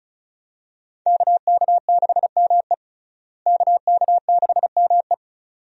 Morse code, heard as a keyed tone, about 34 wpm, 700 Hz. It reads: KK6ME KK6ME
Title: USA Call Signs - 5 Characters - Repeated 2x - 35wpm